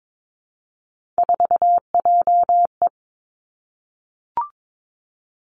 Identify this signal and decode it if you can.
Morse code, heard as a keyed tone, about 22 wpm, 700 Hz. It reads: 4JE
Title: Three Challenging Letter-Number Combinations 22wpm